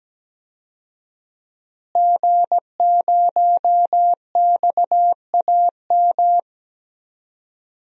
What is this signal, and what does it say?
Morse code, heard as a keyed tone, about 17 wpm, 700 Hz. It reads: G0XAM
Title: DXCC-100 Call Signs - 5 Characters - Repeated 3x - 17wpm